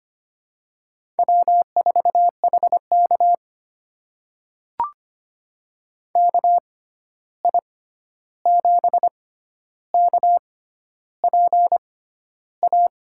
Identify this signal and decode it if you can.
Morse code, heard as a keyed tone, about 25 wpm, spaced as for 11 wpm, 700 Hz. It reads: W4HK KI7KPA
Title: USA Call Signs - 6x Character Spacing - 25wpm